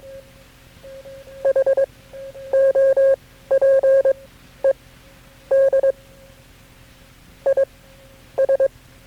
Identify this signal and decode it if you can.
Morse code, heard as a keyed tone, about 22 wpm, spaced as for 11 wpm, 540 Hz. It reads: HOPED IS
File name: KB6MT-tape3-side1